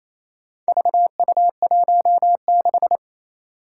Morse code, heard as a keyed tone, about 28 wpm, 700 Hz. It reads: VU16
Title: Four Challenging Letter-Number Combinations 28wpm